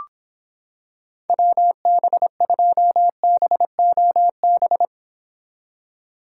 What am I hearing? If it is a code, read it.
Morse code, heard as a keyed tone, about 26 wpm, 700 Hz. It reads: WB2BOB